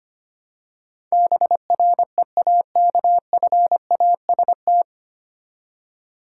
Morse code, heard as a keyed tone, about 25 wpm, 700 Hz. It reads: BREAKFAST
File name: practice / 25wpm / 9 Letter Words - First 3 Letters Given - 25wpm